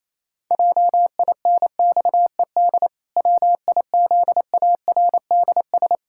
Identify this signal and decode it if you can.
Morse code, heard as a keyed tone, about 27 wpm, 700 Hz. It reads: JINXED WIZARDS
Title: Pangram Challenge v2 - 28wpm